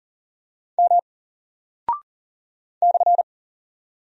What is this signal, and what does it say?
Morse code, heard as a keyed tone, about 40 wpm, 700 Hz. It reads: M /
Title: Single Character (Letter:Number:Punctuation:Prosign) Alt 40wpm